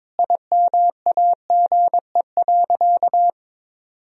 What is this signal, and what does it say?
Morse code, heard as a keyed tone, about 22 wpm, 700 Hz. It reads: IMAGE.
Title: Sentences from Top 1000 Words - Review Part 4 - 22wpm